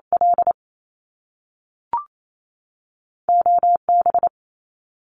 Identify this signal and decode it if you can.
Morse code, heard as a keyed tone, about 28 wpm, spaced as for 3 wpm, 700 Hz. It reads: LO B
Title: Top Two Letter Combinations in Words 28wpm